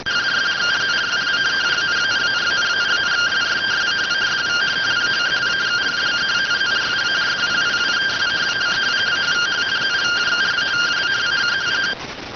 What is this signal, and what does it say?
Two-tone teleprinter signal, about 170 Hz shift.